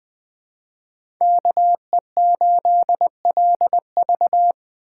Morse code, heard as a keyed tone, about 21 wpm, 700 Hz. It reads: KE8LV
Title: USA Call Signs - 5 Characters - 3x Character Spacing - 20wpm